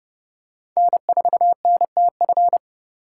Morse code, heard as a keyed tone, about 29 wpm, 700 Hz. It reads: N4NTF